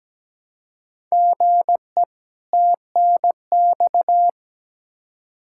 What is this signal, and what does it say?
Morse code, heard as a keyed tone, about 17 wpm, 700 Hz. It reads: GE TNX
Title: Practice QSO - Exchange 1 - Beginning - 17wpm